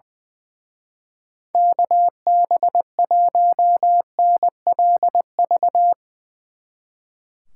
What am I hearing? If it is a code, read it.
Morse code, heard as a keyed tone, about 22 wpm, 700 Hz. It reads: KB1NLV
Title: Speed Racing - USA Call Signs 20wpm